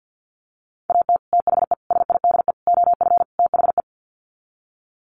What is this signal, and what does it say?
Morse code, heard as a keyed tone, about 50 wpm, 700 Hz. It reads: AT THE SIDE OF THE
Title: Sets of 5 Words Encore2 50wpm